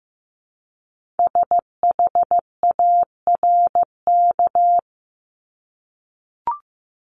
Morse code, heard as a keyed tone, about 15 wpm, 700 Hz. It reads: SHARK
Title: Multi-letter Phonemes 15wpm